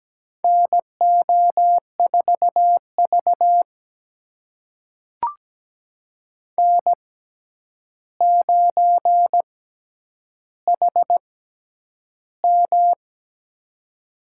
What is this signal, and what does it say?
Morse code, heard as a keyed tone, about 17 wpm, spaced as for 9 wpm, 700 Hz. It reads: NO4V N9HM
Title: USA Call Signs - 4 Characters - 6x Character Spacing - 17wpm